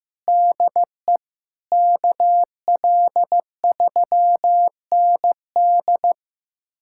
Morse code, heard as a keyed tone, about 15 wpm, 700 Hz. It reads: DE KL3ND
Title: USA Callsign DE Callsign - 5 Characters - 15wpm